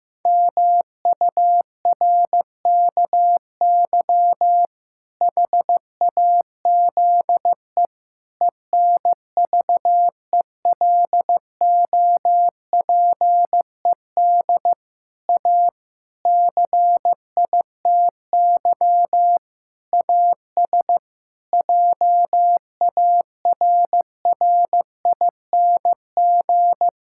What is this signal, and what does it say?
Morse code, heard as a keyed tone, about 14 wpm, 700 Hz. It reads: MURKY HAZE ENVELOPED A CITY AS JARRING
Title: Pangram Challenge v2 - 15wpm